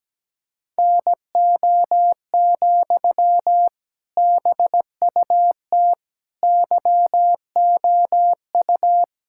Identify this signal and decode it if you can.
Morse code, heard as a keyed tone, about 17 wpm, 700 Hz. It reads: NO, BUT YOU
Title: Sentences from Top 1000 Words - Review Part 6 - 17wpm